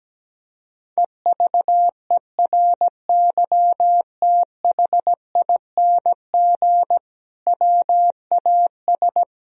Morse code, heard as a keyed tone, about 17 wpm, 700 Hz. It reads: EVERYTHING WAS
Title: Sentences from Top 1000 Words - Review Part 2 - 17wpm